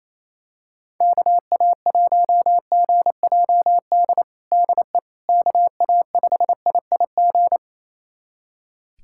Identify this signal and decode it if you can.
Morse code, heard as a keyed tone, about 28 wpm, 700 Hz. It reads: KA1GJD DE KA5IIG
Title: USA Callsign DE Callsign - 6 Characters - 28wpm